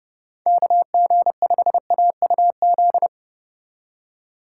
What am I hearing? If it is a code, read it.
Morse code, heard as a keyed tone, about 29 wpm, 700 Hz. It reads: KG5AUZ